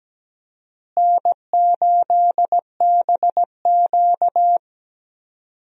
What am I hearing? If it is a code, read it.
Morse code, heard as a keyed tone, about 17 wpm, 700 Hz. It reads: N8BQ
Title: USA Call Signs - 4 Characters - Repeated 3x - 17wpm